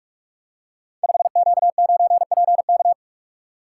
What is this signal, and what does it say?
Morse code, heard as a keyed tone, about 41 wpm, 700 Hz. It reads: 5Q9PK
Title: Code Groups - 5 Letter:Numbers - 45wpm